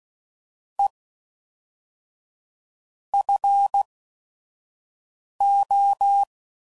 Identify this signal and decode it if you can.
Morse code, heard as a keyed tone, about 16 wpm, 800 Hz. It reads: E F O